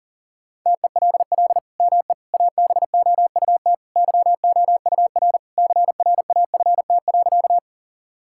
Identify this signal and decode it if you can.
Morse code, heard as a keyed tone, about 40 wpm, 690 Hz. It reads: TELL ME ABOUT YOUR CRAFT.
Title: Sentences from Top 1000 Words - Review Part 6 - 40wpm